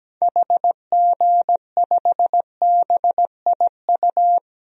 Morse code, heard as a keyed tone, about 17 wpm, 700 Hz. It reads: HG5BIU
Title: DXCC-100 Call Signs - 6 Characters - Repeated 3x - 17wpm